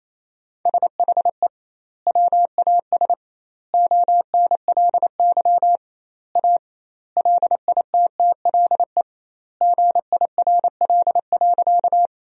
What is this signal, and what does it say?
Morse code, heard as a keyed tone, about 28 wpm, 700 Hz. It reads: SHE WAS ONLY A LITTLE GIRL.
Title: 2x Word-Spacing - Sentences from Top 400 Words - 28wpm